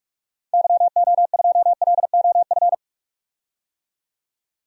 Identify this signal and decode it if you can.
Morse code, heard as a keyed tone, about 46 wpm, 700 Hz. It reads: YO2LOF
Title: DXCC-100 Call Signs - 6 Characters - 45wpm